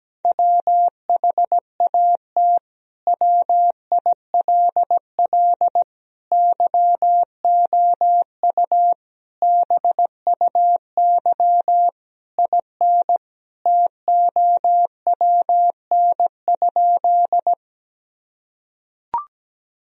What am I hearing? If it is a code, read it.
Morse code, heard as a keyed tone, about 17 wpm, 700 Hz. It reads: WHAT WILL YOU BUY IN TOWN?
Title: Sentences from Top 1000 Words - Review Part 4 - 17wpm